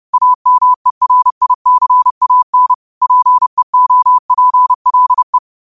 Morse code, heard as a keyed tone, about 30 wpm, 1000 Hz. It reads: AMERICAN PEOPLE
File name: qotdfiles / 30wpm